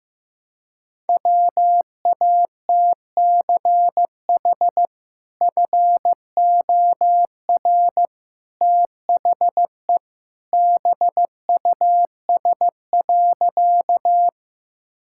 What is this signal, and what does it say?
Morse code, heard as a keyed tone, about 15 wpm, 700 Hz. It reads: WATCH FOR THE BUS.
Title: Sentences from Top 1000 Words - Review Part 6 - 15wpm